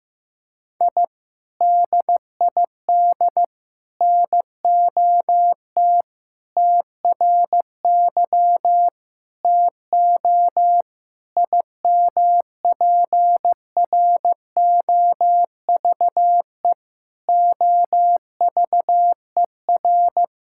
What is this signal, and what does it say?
Morse code, heard as a keyed tone, about 15 wpm, 700 Hz. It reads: I DID NOT TRY TO IMPROVE OVER
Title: Sentences from Top 1000 Words - Review Part 5 - 15wpm